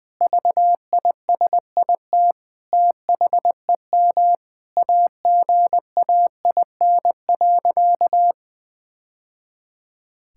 Morse code, heard as a keyed tone, about 21 wpm, 700 Hz. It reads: VISIT THEM AGAIN.
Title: Sentences from Top 1000 Words - Review Part 6 - 20wpm